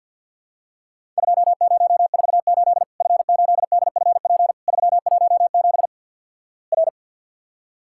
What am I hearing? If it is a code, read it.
Morse code, heard as a keyed tone, about 53 wpm, 700 Hz. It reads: 2048 F8DFP 317 R
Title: Worked All Europe DX Contest - QTC Exchange - 50wpm